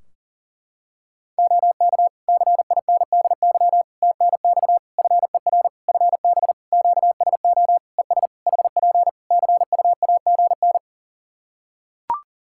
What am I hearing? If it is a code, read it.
Morse code, heard as a keyed tone, about 40 wpm, 700 Hz. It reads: OK CINDY TNX FER FB QSO ES HP CUAGN
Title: K1USN SST QSO Practice 40wpm